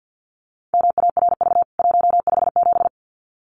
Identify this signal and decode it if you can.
Morse code, heard as a keyed tone, about 48 wpm, 700 Hz. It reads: MARV 157
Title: CWT Exchange 50wpm